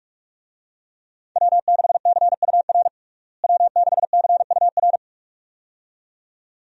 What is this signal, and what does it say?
Morse code, heard as a keyed tone, about 42 wpm, 700 Hz. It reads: W6CUR W6CUR
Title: USA Call Signs - 5 Characters - Repeated 2x - 45wpm